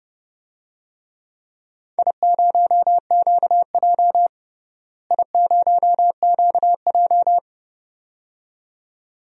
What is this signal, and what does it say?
Morse code, heard as a keyed tone, about 32 wpm, 700 Hz. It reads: I0QJ I0QJ
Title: DXCC-100 Call Signs - 4 Characters - Repeated 2x - 30wpm